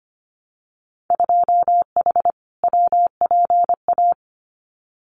Morse code, heard as a keyed tone, about 25 wpm, 700 Hz. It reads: 2H WPA
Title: Winter Field Day Exchanges 25wpm